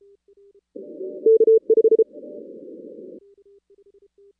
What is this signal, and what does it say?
Morse code, heard as a keyed tone, about 34 wpm, 440 Hz. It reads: K5
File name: 2016 arrl dx cw / 2016-ARRL-DX-CW-K5TR- - 0-17